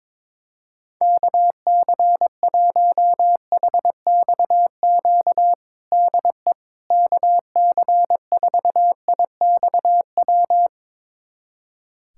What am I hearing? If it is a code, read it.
Morse code, heard as a keyed tone, about 22 wpm, 700 Hz. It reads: KC1HXQ DE KC4IXW
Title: USA Callsign DE Callsign - 6 Characters - 22wpm